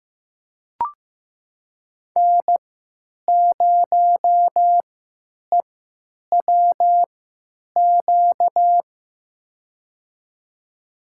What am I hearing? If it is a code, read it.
Morse code, heard as a keyed tone, about 15 wpm, spaced as for 9 wpm, 700 Hz. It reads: N0EWQ
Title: USA Call Signs - 5 Characters - 3x Character Spacing - 15wpm